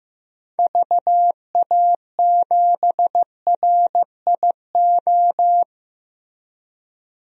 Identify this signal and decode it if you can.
Morse code, heard as a keyed tone, about 15 wpm, 700 Hz. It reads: VA7RIO
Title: DXCC-100 Call Signs - 6 Characters - Repeated 2x - 15wpm